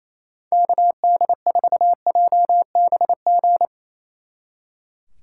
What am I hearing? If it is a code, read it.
Morse code, heard as a keyed tone, about 35 wpm, 700 Hz. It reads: KD4JBG
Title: Speed Racing - USA 2x3 Call Signs 28wpm